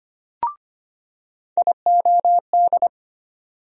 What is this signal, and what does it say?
Morse code, heard as a keyed tone, about 25 wpm, 700 Hz. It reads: IOD
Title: USA Call Sign Postfixes - 3 Letters 25wpm